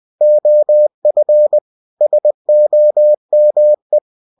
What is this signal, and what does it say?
Morse code, heard as a keyed tone, about 20 wpm, 600 Hz. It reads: OF SOME
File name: cw-books / winnie-the-pooh / winnie-20wpm-page-0048